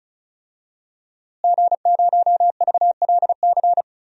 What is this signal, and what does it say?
Morse code, heard as a keyed tone, about 36 wpm, 700 Hz. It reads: G0VLC
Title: DXCC-100 Call Signs - 5 Characters - Repeated 3x - 35wpm